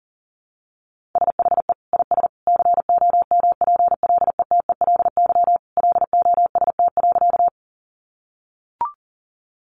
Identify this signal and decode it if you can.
Morse code, heard as a keyed tone, about 37 wpm, 700 Hz. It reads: SHE IS COMPLETELY LOST.
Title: Additional Easy Phrases 40wpm